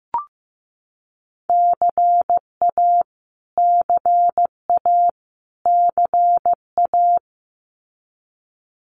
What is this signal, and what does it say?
Morse code, heard as a keyed tone, about 15 wpm, 700 Hz. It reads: CA CA CA
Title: US States Repeated 3x 15wpm